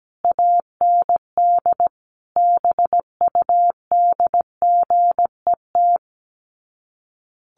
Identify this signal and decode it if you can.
Morse code, heard as a keyed tone, about 17 wpm, 700 Hz. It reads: AND BUDGET
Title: Sets of 5 Words Encore2 17wpm